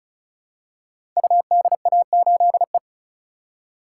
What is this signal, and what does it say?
Morse code, heard as a keyed tone, about 36 wpm, 700 Hz. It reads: UDA8E
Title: Code Groups - 5 Letter:Numbers - 35wpm